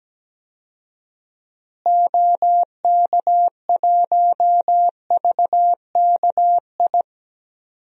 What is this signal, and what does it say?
Morse code, heard as a keyed tone, about 17 wpm, 700 Hz. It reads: OK1VKI
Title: DXCC-100 Call Signs - 6 Characters - Repeated 2x - 17wpm